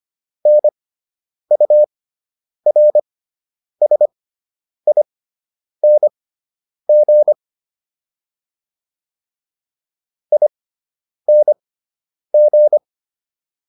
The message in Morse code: NURSING ING

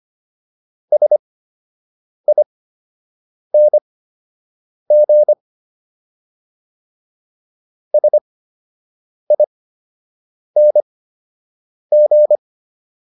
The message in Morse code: SING SING

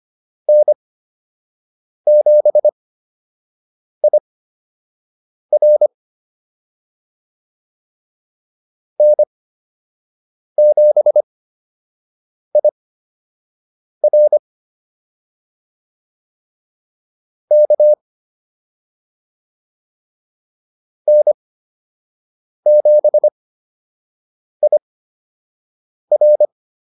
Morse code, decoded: N7IR N7IR K N7IR

